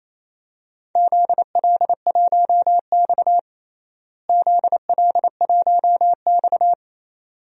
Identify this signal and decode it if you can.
Morse code, heard as a keyed tone, about 28 wpm, 700 Hz. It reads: ZL1X ZL1X
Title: DXCC-100 Call Signs - 4 Characters - Repeated 2x - 28wpm